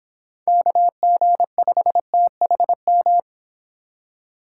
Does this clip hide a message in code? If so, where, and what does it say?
Yes: Morse code from 0.5 to 3.2 s, heard as a keyed tone, about 26 wpm, 700 Hz. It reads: KG5THM